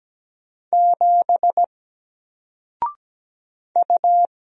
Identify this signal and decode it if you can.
Morse code, heard as a keyed tone, about 17 wpm, 700 Hz. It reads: 7 U
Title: Single Letter-Number Alt 17wpm